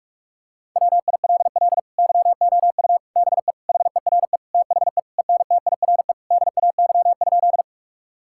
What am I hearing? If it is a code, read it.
Morse code, heard as a keyed tone, about 46 wpm, 700 Hz. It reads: WILL YOU BE HERE THE ENTIRE DAY?
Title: Sentences from Top 1000 Words - Review Part 5 - 45wpm